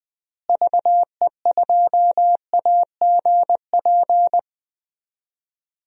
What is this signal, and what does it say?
Morse code, heard as a keyed tone, about 21 wpm, 700 Hz. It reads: VE2AGP
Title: DXCC-100 Call Signs - 6 Characters - Repeated 2x - 20wpm